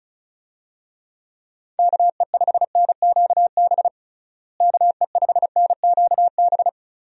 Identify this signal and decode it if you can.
Morse code, heard as a keyed tone, about 36 wpm, 700 Hz. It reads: KE5NQB KE5NQB
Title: USA Call Signs - 6 Characters - Repeated 2x - 35wpm